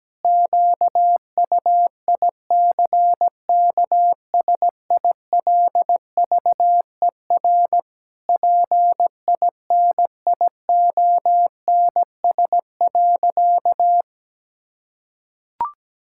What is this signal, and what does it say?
Morse code, heard as a keyed tone, about 17 wpm, 700 Hz. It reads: QUICKSILVER PINIONS.
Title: Pangram Challenge v2 - 17wpm